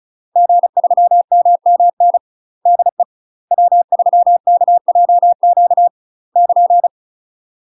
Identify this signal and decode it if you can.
Morse code, heard as a keyed tone, about 36 wpm, 700 Hz. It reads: G3MMN DE W3KJQ <KN>
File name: UK MASTER QSOs TRAINING / 35WPM / UK MASTER QSOs 0007